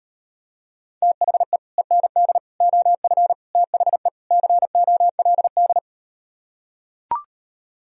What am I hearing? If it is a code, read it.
Morse code, heard as a keyed tone, about 38 wpm, 700 Hz. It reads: THE END OF THE COLD